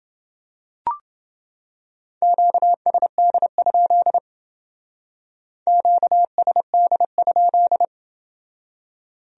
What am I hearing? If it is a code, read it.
Morse code, heard as a keyed tone, about 28 wpm, 700 Hz. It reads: QSD? QSD?